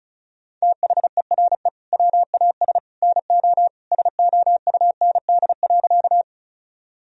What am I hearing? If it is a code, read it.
Morse code, heard as a keyed tone, about 36 wpm, 700 Hz. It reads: THERE WAS NO SOUND.
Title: Sentences from Top 1000 Words - Review Part 1 - 35wpm